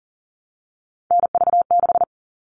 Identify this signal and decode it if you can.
Morse code, heard as a keyed tone, about 40 wpm, 700 Hz. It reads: NV6
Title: All Cut Numbers - 3-Digit 40wpm